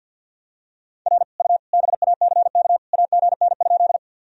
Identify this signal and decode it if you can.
Morse code, heard as a keyed tone, about 49 wpm, 700 Hz. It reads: R U BACK AGN?
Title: Sentences from Top 200 Words and Abbreviations 50wpm